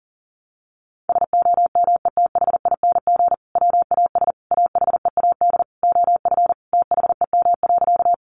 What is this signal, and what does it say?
Morse code, heard as a keyed tone, about 40 wpm, 700 Hz. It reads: SOMETHING WAS AHEAD OF THEM.
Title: Sentences from Top 1100 Words 40wpm